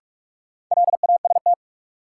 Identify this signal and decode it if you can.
Morse code, heard as a keyed tone, about 46 wpm, 700 Hz. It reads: LAST
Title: Sets of 2 Words - New Word Drill - 45wpm